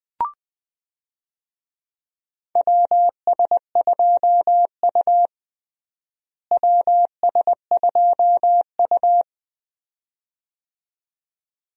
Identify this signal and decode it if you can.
Morse code, heard as a keyed tone, about 20 wpm, 700 Hz. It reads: WS2U WS2U
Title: USA Call Signs - 4 Characters - Repeated 2x - 20wpm